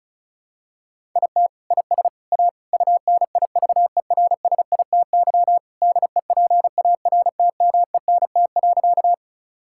Morse code, heard as a keyed tone, about 36 wpm, 700 Hz. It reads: IT IS A UNIVERSITY DEPARTMENT.
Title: Sentences from Top 1000 Words - Review Part 5 - 35wpm